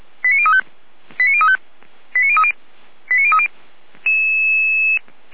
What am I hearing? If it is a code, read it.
Five-tone selective call (ZVEI), heard as quick runs of five tones, tones of about 70 ms. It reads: 89035 89035 89030 89030